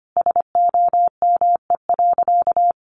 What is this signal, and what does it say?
Morse code, heard as a keyed tone, about 25 wpm, 700 Hz. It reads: SOME.
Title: Sentences from Top 100 Words 25wpm